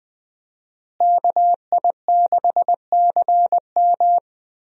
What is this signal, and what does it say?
Morse code, heard as a keyed tone, about 20 wpm, 700 Hz. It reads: KI6CM
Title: USA Call Signs - 5 Characters - Repeated 3x - 20wpm